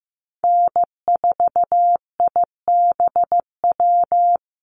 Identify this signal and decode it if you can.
Morse code, heard as a keyed tone, about 15 wpm, 700 Hz. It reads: N4IBW
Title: USA Call Signs Repeated 3x 15wpm